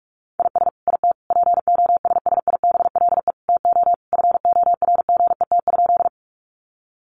Morse code, heard as a keyed tone, about 44 wpm, 700 Hz. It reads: IS IT POSSIBLE TO FORGET?
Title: Sentences from Top 1200 Words 45wpm